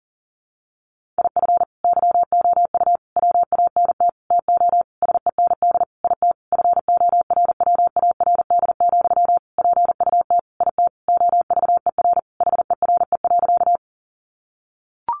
Morse code, heard as a keyed tone, about 40 wpm, 700 Hz. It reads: IF YOU WANT TO SEND IT FORWARD, PUT IT OVER HERE.
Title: Sentences from Top 1500 Words 40wpm